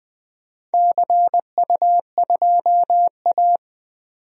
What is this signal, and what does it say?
Morse code, heard as a keyed tone, about 20 wpm, 700 Hz. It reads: CU2A
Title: DXCC-100 Call Signs - 4 Characters - 20wpm